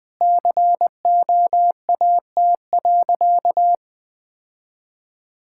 Morse code, heard as a keyed tone, about 20 wpm, 700 Hz. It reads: COAT.
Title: Sentences from Top 1600 Words 20wpm